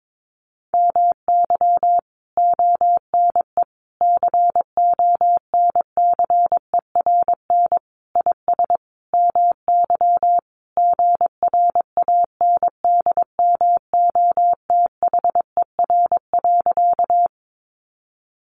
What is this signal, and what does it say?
Morse code, heard as a keyed tone, about 22 wpm, 700 Hz. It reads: MY ONE CONCERN IS MY GRANDMOTHER.
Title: Sentences from Top 1200 Words 22wpm